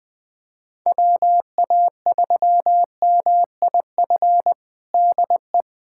Morse code, heard as a keyed tone, about 21 wpm, 700 Hz. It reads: WA3MIF DE
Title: USA Callsign DE Callsign - 6 Characters - 20wpm